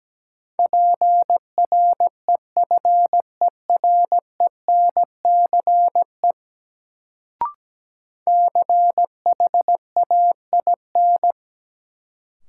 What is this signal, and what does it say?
Morse code, heard as a keyed tone, about 17 wpm, 700 Hz. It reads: PREFERENCE CHAIN
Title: Sentences from Top 1500 Words - New Word Drill Alt - 17wpm